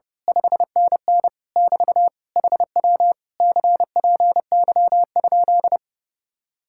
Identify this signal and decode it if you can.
Morse code, heard as a keyed tone, about 28 wpm, 700 Hz. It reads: 5NN <BT> HW CPY?
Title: Speed-Racing Common QSO Phrases - 30wpm